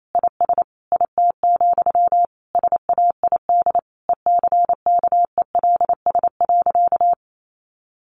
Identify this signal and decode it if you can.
Morse code, heard as a keyed tone, about 28 wpm, 700 Hz. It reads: IS IT, SAID ECKELS.
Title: A Sound Of Thunder 28wpm